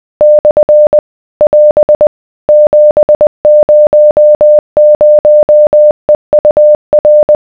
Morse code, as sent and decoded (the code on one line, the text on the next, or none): -..-. / .-... / --... ----- ----- . ..- .-.
/ <AS> 700EUR